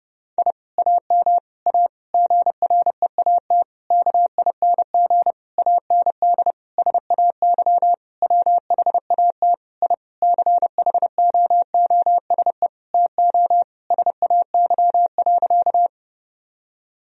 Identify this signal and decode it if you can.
Morse code, heard as a keyed tone, about 32 wpm, 700 Hz. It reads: I AM A GREAT KING AND SAY WHAT I CHOOSE TO SAY.
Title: Sentences from Top 1000 Words - Review Part 4 - 30wpm